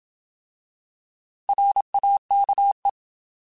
Morse code, heard as a keyed tone, about 27 wpm, 800 Hz. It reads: RAKE
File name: Alphabetical / Alphabetical mp3s / Rs @25